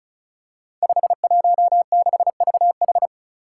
Morse code, heard as a keyed tone, about 36 wpm, 700 Hz. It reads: 516VH
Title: Five Challenging Letter-Number Combinations 35wpm